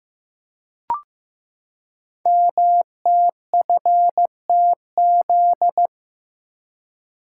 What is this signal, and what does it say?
Morse code, heard as a keyed tone, about 15 wpm, 700 Hz. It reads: MTFTZ
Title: Code Groups - 5 Letter:Numbers - 15wpm